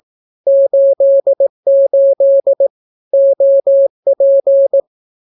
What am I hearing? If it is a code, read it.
Morse code, heard as a keyed tone, about 18 wpm, 550 Hz. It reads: 88 OP
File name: naukapodstaw / skroty / skroty